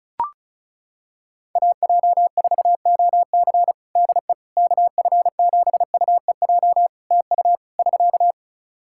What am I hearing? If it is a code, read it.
Morse code, heard as a keyed tone, about 36 wpm, 700 Hz. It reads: AJ4OC DE KF7UEJ TU <SK>
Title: Practice QSO - Exchange 4 - Ending - 35wpm